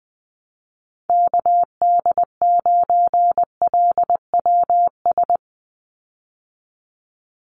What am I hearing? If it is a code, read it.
Morse code, heard as a keyed tone, about 20 wpm, 700 Hz. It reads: KD9LWS
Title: USA Call Signs 20wpm